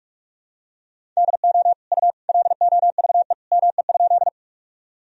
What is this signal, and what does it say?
Morse code, heard as a keyed tone, about 43 wpm, 700 Hz. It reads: DO U LOVE ME?
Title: Sentences from Top 400 Words and Abbreviations 45wpm